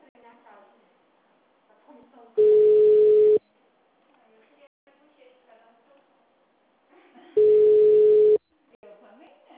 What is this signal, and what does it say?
Ringback tone (Europe, 425 Hz), with speech.
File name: out- -2005- - - .31237